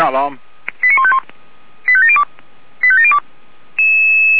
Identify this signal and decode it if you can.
Five-tone selective call (ZVEI), heard as quick runs of five tones, tones of about 70 ms, with speech. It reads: 89151 85702 85702